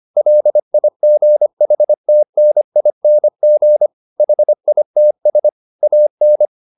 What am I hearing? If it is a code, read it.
Morse code, heard as a keyed tone, about 25 wpm, 600 Hz. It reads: LIGHTNING HITS AN